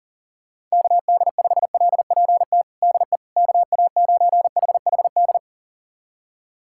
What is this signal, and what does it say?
Morse code, heard as a keyed tone, about 40 wpm, 700 Hz. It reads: KD5LPT DE KA9HHD
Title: USA Callsign DE Callsign 40wpm